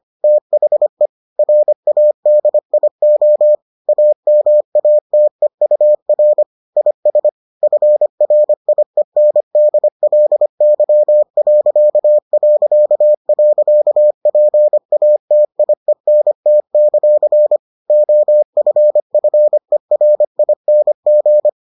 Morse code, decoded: THE RADIO AMATEUR IS FRIENDLY...PATIENT; OFFERING

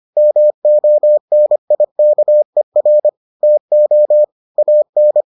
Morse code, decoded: MONIKER TO AN

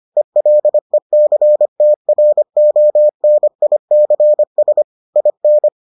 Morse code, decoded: ELECTRONICS IN

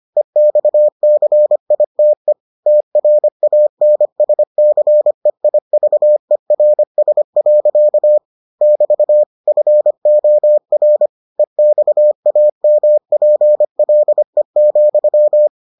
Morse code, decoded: EXCITE TRANSCEIVERS. <BT> FOR EXAMPLE,

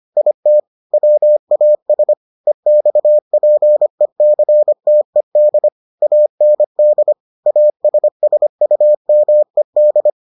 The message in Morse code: IT WAS EXPECTED AND ASSUMED

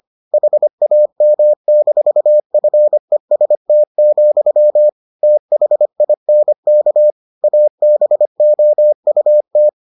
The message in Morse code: HAM-FEST, THINK ABOUT

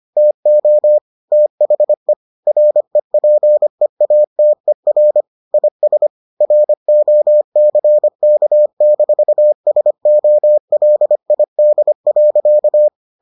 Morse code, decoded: TO THE REPEATER IS ROCK-SOLID.

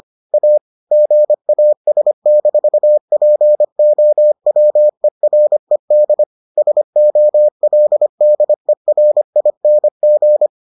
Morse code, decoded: A GAS-POWERED SOLDERING

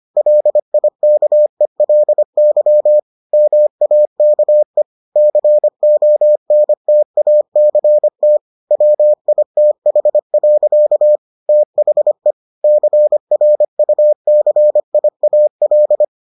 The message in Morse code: LIKELY MAKE CONTACT WITH. THE CRUCIAL

